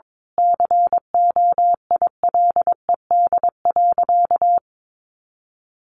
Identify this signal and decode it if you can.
Morse code, heard as a keyed tone, about 22 wpm, 700 Hz. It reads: COILED.
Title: A Sound Of Thunder 22wpm